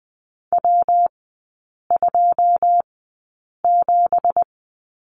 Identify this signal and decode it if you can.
Morse code, heard as a keyed tone, about 20 wpm, spaced as for 8 wpm, 690 Hz. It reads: W27
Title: 166 Lesson 27e - Number 7 - Three Characters 20wpm